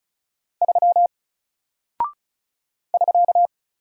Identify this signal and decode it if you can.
Morse code, heard as a keyed tone, about 34 wpm, 700 Hz. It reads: 3 <SK>
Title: Single Character (Letter:Number:Punctuation:Prosign) Alt 35wpm